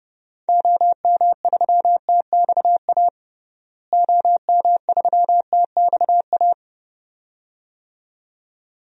Morse code, heard as a keyed tone, about 30 wpm, 700 Hz. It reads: OM3TXA OM3TXA
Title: DXCC-100 Call Signs - 6 Characters - Repeated 2x - 30wpm